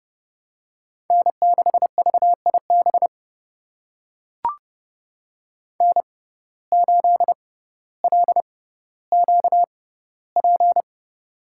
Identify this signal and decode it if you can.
Morse code, heard as a keyed tone, about 32 wpm, spaced as for 16 wpm, 700 Hz. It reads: N6VIB N8LQP
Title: USA Call Signs - 5 Characters - 6x Character Spacing - 30wpm